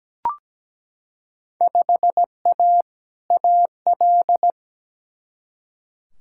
Morse code, heard as a keyed tone, about 17 wpm, 700 Hz. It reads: 5A AL
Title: Field Day Exchanges Encore - 17wpm